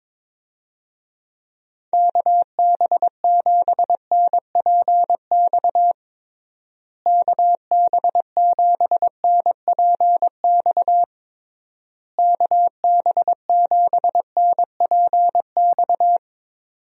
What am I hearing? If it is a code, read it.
Morse code, heard as a keyed tone, about 22 wpm, 700 Hz. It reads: KB7NPX KB7NPX KB7NPX
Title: USA Call Signs - 6 Characters - Repeated 3x - 22wpm